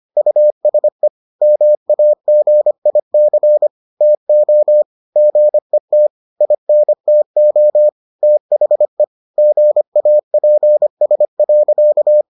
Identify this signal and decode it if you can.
Morse code, heard as a keyed tone, about 25 wpm, 600 Hz. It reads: USE MAGIC TO GET INTO THE GAPS.